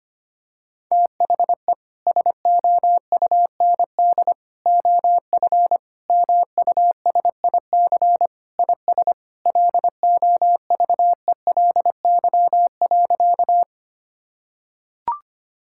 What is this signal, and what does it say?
Morse code, heard as a keyed tone, about 25 wpm, 700 Hz. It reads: THE SOUND OF MUSIC IS LOVELY.
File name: practice / 25wpm / Sentences with 3-Word Phrases - 25wpm